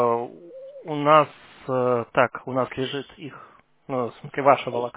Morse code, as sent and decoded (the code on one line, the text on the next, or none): none